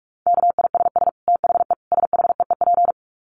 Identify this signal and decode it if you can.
Morse code, heard as a keyed tone, about 44 wpm, 700 Hz. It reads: KISS THE SHEEP
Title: Dit Challenge 45wpm